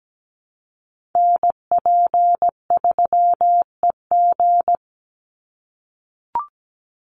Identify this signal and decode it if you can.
Morse code, heard as a keyed tone, about 17 wpm, 700 Hz. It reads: NP3EG
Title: USA Call Signs 17wpm